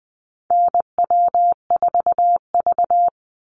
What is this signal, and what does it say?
Morse code, heard as a keyed tone, about 20 wpm, 700 Hz. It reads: NW4V
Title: Call Signs Repeated 3x - 20wpm